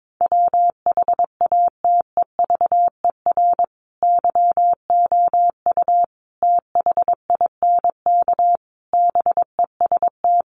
Morse code, heard as a keyed tone, about 22 wpm, 700 Hz. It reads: WHATEVER YOU THINK BEST
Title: Sentences from Top 1100 Words 22wpm